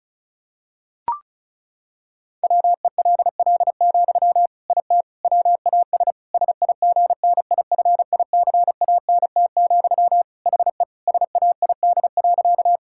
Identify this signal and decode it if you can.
Morse code, heard as a keyed tone, about 36 wpm, 700 Hz. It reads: WELL, IT WAS SIGNIFICANT, HE SAID.
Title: Sentences from Top 1000 Words - Review Part 6 - 35wpm